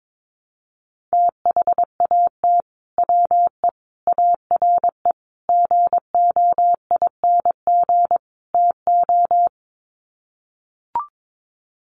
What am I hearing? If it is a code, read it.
Morse code, heard as a keyed tone, about 22 wpm, 700 Hz. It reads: THAT WE ARE GOING TO
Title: Sets of 5 Words Encore 22wpm